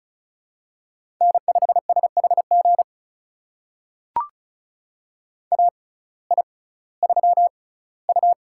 Morse code, heard as a keyed tone, about 36 wpm, spaced as for 19 wpm, 700 Hz. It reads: N5SHG AI3U
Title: USA Call Signs - 6x Character Spacing - 35wpm